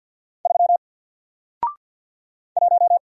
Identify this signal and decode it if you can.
Morse code, heard as a keyed tone, about 54 wpm, 700 Hz. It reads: 3 1
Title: Single Letter-Number Alt 50wpm